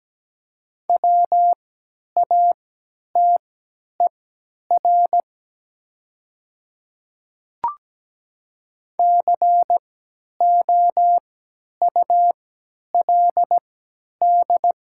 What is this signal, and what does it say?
Morse code, heard as a keyed tone, about 17 wpm, spaced as for 10 wpm, 700 Hz. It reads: WATER COULD
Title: 3x Character Spacing - Top 100 Words - 17wpm